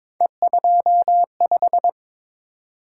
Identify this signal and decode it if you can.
Morse code, heard as a keyed tone, about 22 wpm, 700 Hz. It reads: E25
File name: practice / 22wpm / Three Letter-Number Combinations - 22wpm